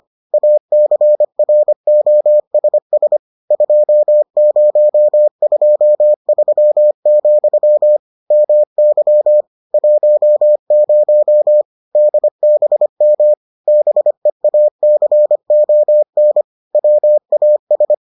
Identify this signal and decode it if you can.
Morse code, heard as a keyed tone, about 25 wpm, 600 Hz. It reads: ACROSS 2023, MY 10 DBM BEACON WAS